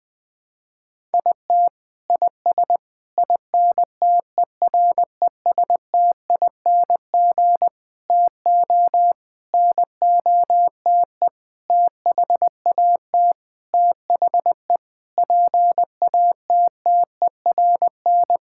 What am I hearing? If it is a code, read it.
Morse code, heard as a keyed tone, about 20 wpm, 700 Hz. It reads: IT IS INTERESTING TO NOTE THAT THE PATTERN
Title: Sentences with 6-Word Phrases - Spoken Beginning - 20wpm